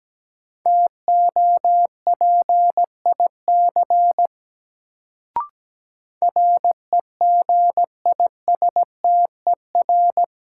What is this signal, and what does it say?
Morse code, heard as a keyed tone, about 17 wpm, 700 Hz. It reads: TOPIC REGISTER
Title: Sentences from Top 1200 Words - New Word Drill - 17wpm